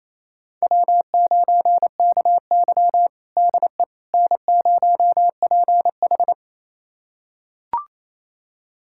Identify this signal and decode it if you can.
Morse code, heard as a keyed tone, about 28 wpm, 700 Hz. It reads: W9KY DE N0PH
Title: USA Callsign DE Callsign - 4 Characters - 28wpm